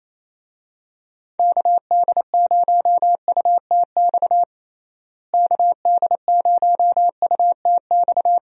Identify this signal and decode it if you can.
Morse code, heard as a keyed tone, about 28 wpm, 700 Hz. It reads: KD0UTX KD0UTX
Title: USA Call Signs - 6 Characters - Repeated 2x - 28wpm